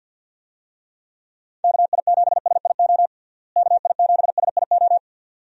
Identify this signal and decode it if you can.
Morse code, heard as a keyed tone, about 51 wpm, 700 Hz. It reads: KI7SIO KI7SIO
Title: USA Call Signs - 6 Characters - Repeated 2x - 50wpm